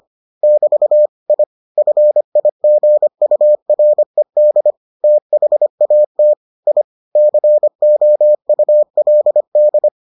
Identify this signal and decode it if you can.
Morse code, heard as a keyed tone, about 25 wpm, 600 Hz. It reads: <BT> I FIGURED THAT I COULD